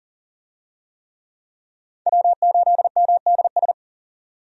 Morse code, heard as a keyed tone, about 40 wpm, 700 Hz. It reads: W8MDS
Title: USA Call Signs - 5 Characters - 40wpm